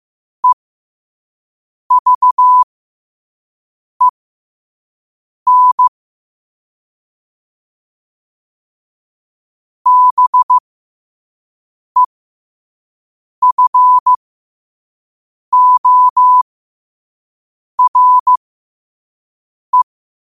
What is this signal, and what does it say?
Morse code, heard as a keyed tone, about 15 wpm, spaced as for 5 wpm, 1000 Hz. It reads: EVEN BEFORE
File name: qotdfiles / 5wpm